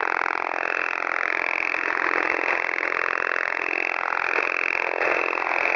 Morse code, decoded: none